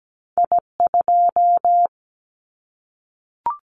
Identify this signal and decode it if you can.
Morse code, heard as a keyed tone, about 17 wpm, 700 Hz. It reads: I2
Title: Callsign Prefixes for Top 25 DXCC - 17wpm